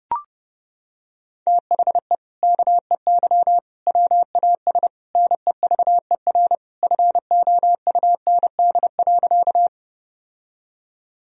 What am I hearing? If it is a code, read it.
Morse code, heard as a keyed tone, about 30 wpm, 700 Hz. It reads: THE KEY WAS NEVER FOUND.
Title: Sentences from Top 1000 Words - Review Part 4 - 30wpm